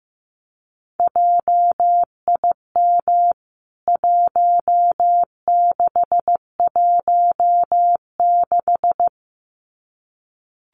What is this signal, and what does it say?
Morse code, heard as a keyed tone, about 15 wpm, 700 Hz. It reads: JIM 1616
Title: CWT Exchange 15wpm